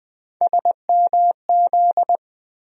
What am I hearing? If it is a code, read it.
Morse code, heard as a keyed tone, about 20 wpm, 700 Hz. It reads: SMZ
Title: USA Call Sign Postfixes - 3 Letters 20wpm